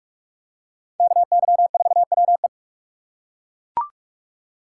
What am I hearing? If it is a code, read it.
Morse code, heard as a keyed tone, about 40 wpm, 700 Hz. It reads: KY4WE